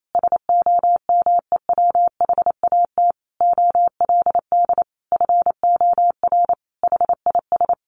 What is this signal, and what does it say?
Morse code, heard as a keyed tone, about 28 wpm, 700 Hz. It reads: SOMEWHAT OLD FOR HIS
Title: Sentences from Top 1300 Words 28wpm